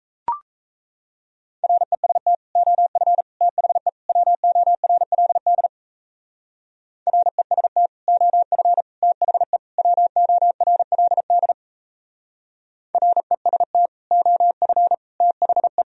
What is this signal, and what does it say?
Morse code, heard as a keyed tone, about 36 wpm, 700 Hz. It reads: REST OF THE WORLD REST OF THE WORLD REST OF THE